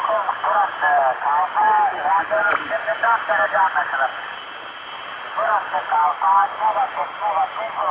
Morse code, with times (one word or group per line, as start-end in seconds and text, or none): none